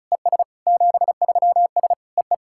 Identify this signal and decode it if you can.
Morse code, heard as a keyed tone, about 34 wpm, 700 Hz. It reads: ES 73S EE
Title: SOTA QSO Practice 35wpm